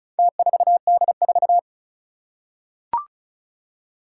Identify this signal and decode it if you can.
Morse code, heard as a keyed tone, about 36 wpm, 700 Hz. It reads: T4D4
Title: Four Challenging Letter-Number Combinations 35wpm